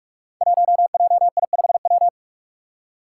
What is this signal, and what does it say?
Morse code, heard as a keyed tone, about 44 wpm, 700 Hz. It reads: 1JI5W
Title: Five Challenging Letter-Number Combinations 45wpm